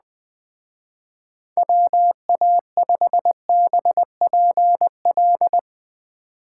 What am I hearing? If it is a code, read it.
Morse code, heard as a keyed tone, about 20 wpm, 700 Hz. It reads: WA5BPL
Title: Speed Racing - USA Call Signs 17wpm